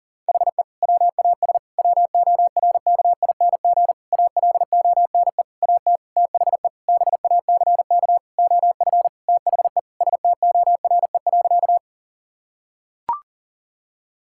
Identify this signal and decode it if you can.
Morse code, heard as a keyed tone, about 40 wpm, 700 Hz. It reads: HE WAS WORKING ALONE AT THE BACK OF THE STORE.
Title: Sentences from Top 1000 Words - Review Part 5 - 40wpm